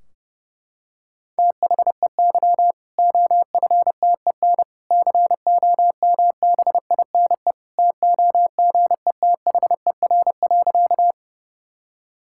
Morse code, heard as a keyed tone, about 29 wpm, 700 Hz. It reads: THEY OFTEN COMBINE TOGETHER.
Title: Sentences from Top 1600 Words 30wpm